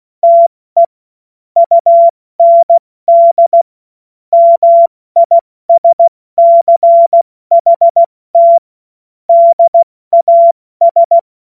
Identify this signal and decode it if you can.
Morse code, heard as a keyed tone, about 16 wpm, 680 Hz. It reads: TE UND MISCHT DAS
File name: Morsekurs / Witze / Witz 09 April 2024 70 F80